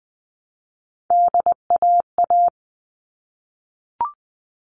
Morse code, heard as a keyed tone, about 20 wpm, 700 Hz. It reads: DAA
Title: All Cut Numbers - 3-Digit 20wpm